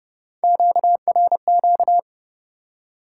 Morse code, heard as a keyed tone, about 30 wpm, 700 Hz. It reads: QRQ
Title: Top 90 QSO Elements 30wpm